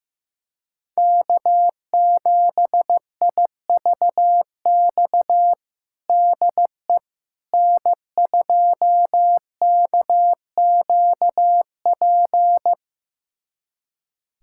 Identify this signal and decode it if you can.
Morse code, heard as a keyed tone, about 16 wpm, 700 Hz. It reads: K7IVX DE N2KQP
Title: USA Callsign DE Callsign - 5 Characters - 15wpm